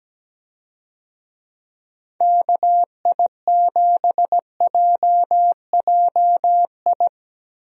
Morse code, heard as a keyed tone, about 17 wpm, 700 Hz. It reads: KI7JJI
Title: USA Call Signs - 6 Characters - 17wpm